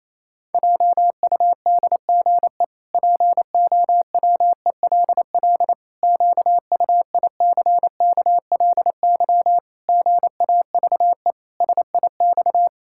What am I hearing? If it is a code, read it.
Morse code, heard as a keyed tone, about 28 wpm, 700 Hz. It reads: JUDGE POWELL QUICKLY GAVE SIX
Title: Pangram Challenge v2 - 28wpm